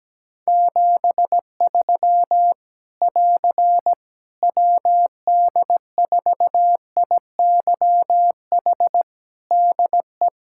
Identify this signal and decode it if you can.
Morse code, heard as a keyed tone, about 17 wpm, 700 Hz. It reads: 73 <AR> WD4IYH DE
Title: Practice QSO - Exchange 4 - Ending - 17wpm